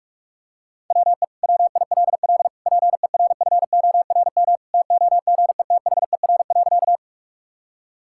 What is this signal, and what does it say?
Morse code, heard as a keyed tone, about 41 wpm, 690 Hz. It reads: WE WILL PERFORM TOGETHER.
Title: Sentences from Top 1000 Words - Review Part 5 - 45wpm